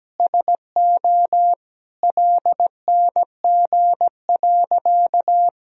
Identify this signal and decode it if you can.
Morse code, heard as a keyed tone, about 17 wpm, 700 Hz. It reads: SO LNG.
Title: Sentences from Top 100 Words and Abbreviations 17wpm